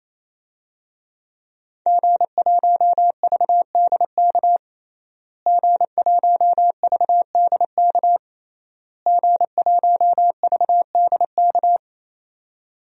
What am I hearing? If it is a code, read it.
Morse code, heard as a keyed tone, about 27 wpm, 700 Hz. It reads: G1VDK G1VDK G1VDK
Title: DXCC-100 Call Signs - 5 Characters - Repeated 3x - 28wpm